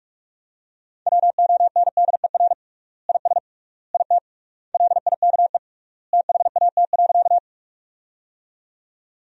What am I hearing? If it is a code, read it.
Morse code, heard as a keyed tone, about 41 wpm, 700 Hz. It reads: WONDER IS IT LIKE THAT.
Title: 3x Word-Spacing - Sentences from Top 500 Words - 45wpm